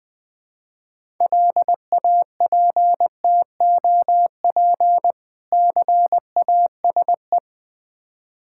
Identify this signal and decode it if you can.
Morse code, heard as a keyed tone, about 20 wpm, 700 Hz. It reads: LAPTOP CASE
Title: Difficult Words with Hints - 20wpm